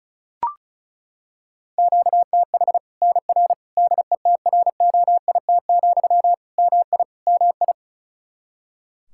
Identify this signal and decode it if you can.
Morse code, heard as a keyed tone, about 34 wpm, 700 Hz. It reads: QTH NR DETROIT, MI MI
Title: Practice QSO - Exchange 1 - Beginning - 35wpm